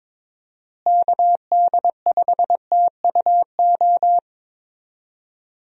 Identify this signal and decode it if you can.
Morse code, heard as a keyed tone, about 22 wpm, 700 Hz. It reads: KD5TUO
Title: USA Call Signs - 6 Characters - Repeated 2x - 22wpm